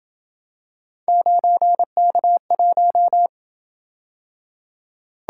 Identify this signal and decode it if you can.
Morse code, heard as a keyed tone, about 27 wpm, 700 Hz. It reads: 9K1